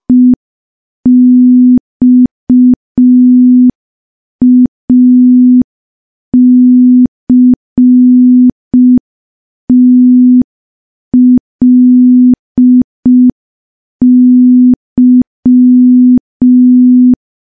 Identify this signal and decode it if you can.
Morse code, heard as a keyed tone, about 5 wpm, 250 Hz. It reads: EXACTLY
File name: morse / morse-training-05WPM